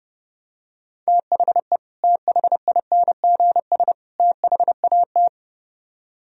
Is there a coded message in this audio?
Yes: Morse code from 1.1 to 5.3 s, heard as a keyed tone, about 30 wpm, 700 Hz. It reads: THE THINGS THAT